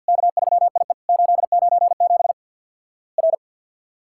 Morse code, heard as a keyed tone, about 48 wpm, 700 Hz. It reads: K3IE 897 R
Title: Worked All Europe DX Contest - QTC Exchange with Repeats - 50wpm